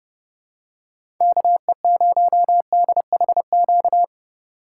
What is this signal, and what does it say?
Morse code, heard as a keyed tone, about 30 wpm, 700 Hz. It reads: KE0DHQ
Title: USA Call Signs - 6 Characters - 3x Character Spacing - 30wpm